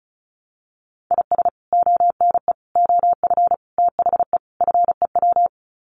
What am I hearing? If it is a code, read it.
Morse code, heard as a keyed tone, about 34 wpm, 700 Hz. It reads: IS ONE OF THE FEW
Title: Sets of 5 Words 35wpm